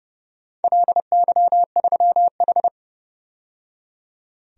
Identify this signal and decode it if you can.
Morse code, heard as a keyed tone, about 32 wpm, 700 Hz. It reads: LY3H
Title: DXCC-100 Call Signs - 4 Characters - 30wpm